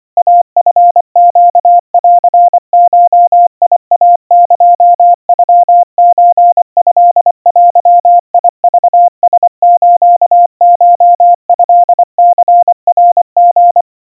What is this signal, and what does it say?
Morse code (Japanese Wabun code), heard as a keyed tone, about 25 wpm, 700 Hz. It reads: イチネンゴイエノソトデクラスコトニナリ
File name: 90 cpm 700 Hz Crispy